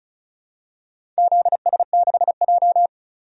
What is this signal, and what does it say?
Morse code, heard as a keyed tone, about 34 wpm, 700 Hz. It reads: ZS6J
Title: DXCC-100 Call Signs - 4 Characters - 35wpm